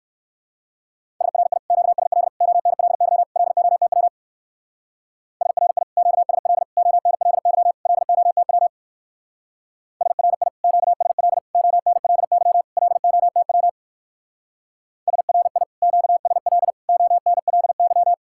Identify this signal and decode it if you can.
Morse code, heard as a keyed tone, about 59 wpm, 700 Hz. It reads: SRI QSL ONLY LOTW SRI QSL ONLY LOTW SRI QSL ONLY LOTW SRI QSL ONLY